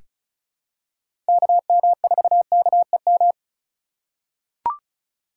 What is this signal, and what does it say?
Morse code, heard as a keyed tone, about 36 wpm, 700 Hz. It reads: KM4KEM
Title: USA Call Signs - 6 Characters - 6x Character Spacing - 35wpm